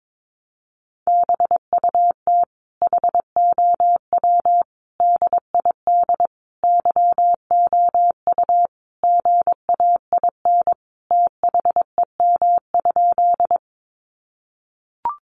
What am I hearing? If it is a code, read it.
Morse code, heard as a keyed tone, about 22 wpm, 700 Hz. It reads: BUT HOW DID YOU GAIN THEM?
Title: Sentences from Top 1000 Words 22wpm